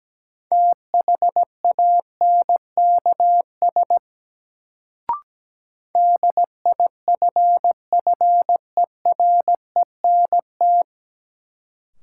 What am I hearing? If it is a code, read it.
Morse code, heard as a keyed tone, about 17 wpm, 700 Hz. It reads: THANKS DIFFERENT
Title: Sentences from Top 600 Words - New Word Drill - 17wpm